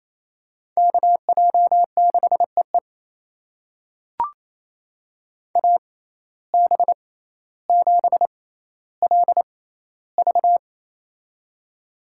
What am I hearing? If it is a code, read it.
Morse code, heard as a keyed tone, about 28 wpm, spaced as for 14 wpm, 700 Hz. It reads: KJ6EE AB7LV
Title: USA Call Signs - 5 Characters - 6x Character Spacing - 28wpm